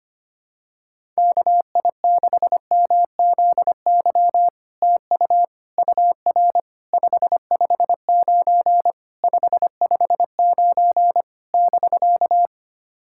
Morse code, heard as a keyed tone, about 25 wpm, 700 Hz. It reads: KI6MZY TU UR 559 559 <BK>
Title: SOTA QSO Practice 25wpm